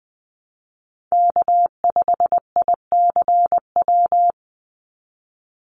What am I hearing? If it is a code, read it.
Morse code, heard as a keyed tone, about 20 wpm, 700 Hz. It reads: K5ICW
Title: USA Call Signs 20wpm